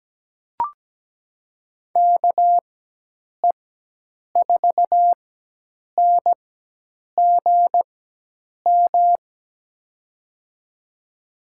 Morse code, heard as a keyed tone, about 17 wpm, spaced as for 8 wpm, 700 Hz. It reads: KE4NGM
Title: USA Call Signs - 6 Characters - 4x Character Spacing - 17wpm